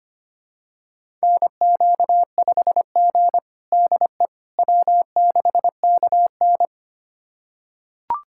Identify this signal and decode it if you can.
Morse code, heard as a keyed tone, about 25 wpm, 700 Hz. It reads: NQ5G DE W6KN
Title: USA Callsign DE Callsign - 4 Characters - 25wpm